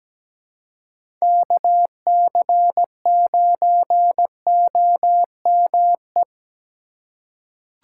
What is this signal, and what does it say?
Morse code, heard as a keyed tone, about 17 wpm, 700 Hz. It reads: KC9OME
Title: USA Call Signs - 6 Characters - 4x Character Spacing - 17wpm